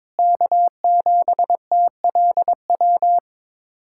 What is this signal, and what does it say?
Morse code, heard as a keyed tone, about 22 wpm, 700 Hz. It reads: K7TLW
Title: USA Callsign DE Callsign - 5 Characters - 22wpm